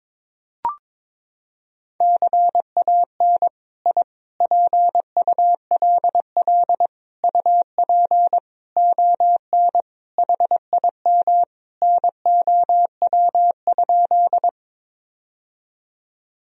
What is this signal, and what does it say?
Morse code, heard as a keyed tone, about 22 wpm, 700 Hz. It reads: CAN I PULL UP ON HIM NOW?
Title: Sentences from Top 1000 Words - Review Part 2 - 22wpm